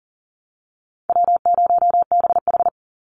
Morse code, heard as a keyed tone, about 40 wpm, 700 Hz. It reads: W0BH
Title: Call Signs - 40wpm